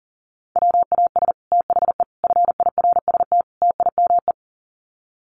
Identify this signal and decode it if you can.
Morse code, heard as a keyed tone, about 40 wpm, 700 Hz. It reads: WAS THE FIRST TIME
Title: Sets of 4 Words 40wpm